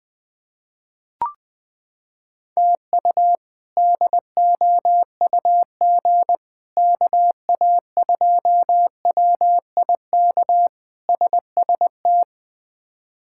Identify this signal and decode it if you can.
Morse code, heard as a keyed tone, about 20 wpm, 700 Hz. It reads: TU DOUG KA2WIK SST
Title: K1USN SST QSO Practice - 20wpm